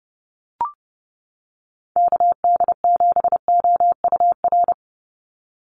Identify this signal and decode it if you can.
Morse code, heard as a keyed tone, about 30 wpm, 700 Hz. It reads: KD7OUR
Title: USA Call Signs 30wpm